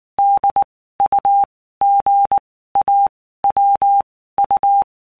Morse code, heard as a keyed tone, about 19 wpm, 800 Hz. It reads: D U G A W U
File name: Basic Letter Groups By Blocks / Groups@18 / AWUNDG@18